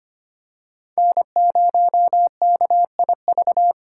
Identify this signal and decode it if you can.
Morse code, heard as a keyed tone, about 25 wpm, 700 Hz. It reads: N0KIV
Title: USA Call Signs - 5 Characters - 3x Character Spacing - 25wpm